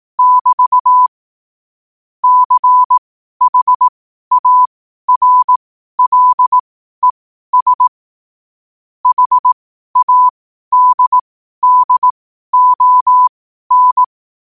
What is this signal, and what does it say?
Morse code, heard as a keyed tone, about 18 wpm, spaced as for 12 wpm, 1000 Hz. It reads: <BT> CHARLES HADDON
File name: qotdfiles / 12wpm